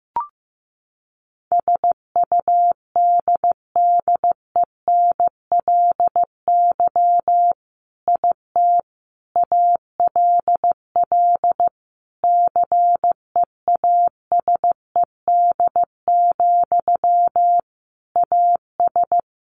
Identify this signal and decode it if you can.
Morse code, heard as a keyed tone, about 15 wpm, 700 Hz. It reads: SUDDENLY IT ALL CEASED, AS
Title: A Sound Of Thunder 15wpm